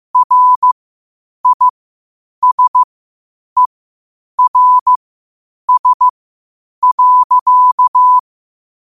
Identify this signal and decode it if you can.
Morse code, heard as a keyed tone, about 15 wpm, spaced as for 8 wpm, 1000 Hz. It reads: RISERS.
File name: qotdfiles / 8wpm